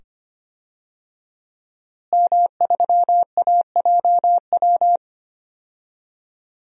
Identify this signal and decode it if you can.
Morse code, heard as a keyed tone, about 25 wpm, 700 Hz. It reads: M3AJW
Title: DXCC-100 Call Signs - 5 Characters - Repeated 2x - 25wpm